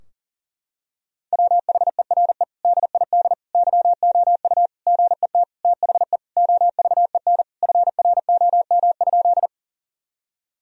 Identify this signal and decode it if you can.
Morse code, heard as a keyed tone, about 40 wpm, 700 Hz. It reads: WHERE DID YOU GET THE OVEN FROM?
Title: Sentences from Top 1000 Words - Review Part 4 - 40wpm